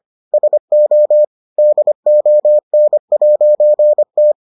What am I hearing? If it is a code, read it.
Morse code, heard as a keyed tone, about 25 wpm, 600 Hz. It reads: SO DON'T